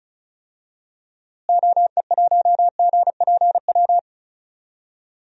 Morse code, heard as a keyed tone, about 36 wpm, 700 Hz. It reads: OE1GPW
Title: DXCC-100 Call Signs - 6 Characters - Repeated 3x - 35wpm